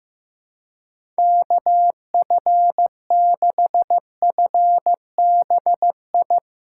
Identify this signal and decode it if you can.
Morse code, heard as a keyed tone, about 15 wpm, 700 Hz. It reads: KF6FBI
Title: USA Call Signs - 6x Character Spacing - 15wpm